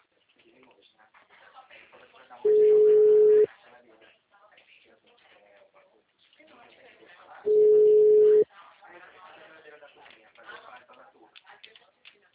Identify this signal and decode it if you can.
Ringback tone (Europe, 425 Hz), with speech.